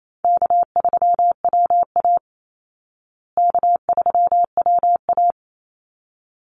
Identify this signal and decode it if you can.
Morse code, heard as a keyed tone, about 28 wpm, 700 Hz. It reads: K3WA K3WA
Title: Easy Call Signs Repeated 2x - 28wpm